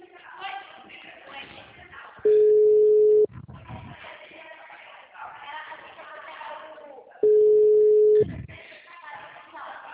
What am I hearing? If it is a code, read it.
Ringback tone (Europe, 425 Hz), with speech.
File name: out- -2021- - - .68033